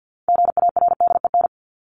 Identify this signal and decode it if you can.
Morse code, heard as a keyed tone, about 50 wpm, 700 Hz. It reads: GARDEN
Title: Top 501-750 Words 50wpm